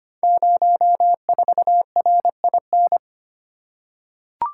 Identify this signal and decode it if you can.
Morse code, heard as a keyed tone, about 25 wpm, 700 Hz. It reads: 04RIN
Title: Code Groups - 5 Letter:Numbers - 25wpm